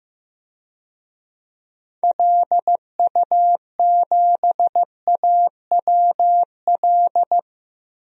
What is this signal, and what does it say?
Morse code, heard as a keyed tone, about 16 wpm, 700 Hz. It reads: LU7AWL
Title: DXCC-100 Call Signs - 6 Characters - 15wpm